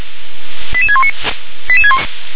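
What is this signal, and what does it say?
Five-tone selective call (ZVEI), heard as quick runs of five tones, tones of about 70 ms. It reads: 88619
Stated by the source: Received by SDR; ZVEI five-tone selective call.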